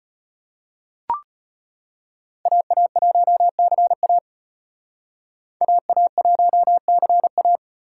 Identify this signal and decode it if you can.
Morse code, heard as a keyed tone, about 31 wpm, 700 Hz. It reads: AA1CA AA1CA
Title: Speed Racing - USA 2x2 Call Signs 25wpm